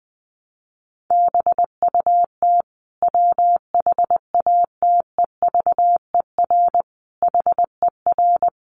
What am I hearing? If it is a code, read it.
Morse code, heard as a keyed tone, about 19 wpm, 700 Hz. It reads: BUT WHATEVER HER
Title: Sentences from Top 1100 Words 20wpm